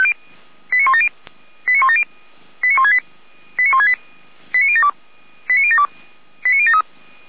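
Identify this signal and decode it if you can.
Five-tone selective call (ZVEI), heard as quick runs of five tones, tones of about 70 ms. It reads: 89170 89170 89168 89168 89072 89072 89073
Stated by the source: ZVEI five-tone call received on SDR.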